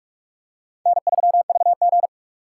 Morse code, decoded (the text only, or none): N3VG